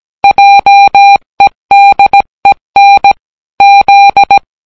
Morse code, heard as a keyed tone, about 17 wpm, 800 Hz. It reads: JEDEN Z